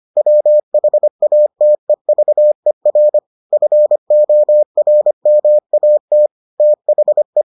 Morse code, heard as a keyed tone, about 25 wpm, 600 Hz. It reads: WHATEVER FORMAT THE